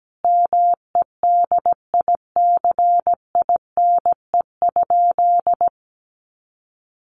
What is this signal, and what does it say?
Morse code, heard as a keyed tone, about 17 wpm, 700 Hz. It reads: MEDICINE?
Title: Sentences from Top 900 Words 17wpm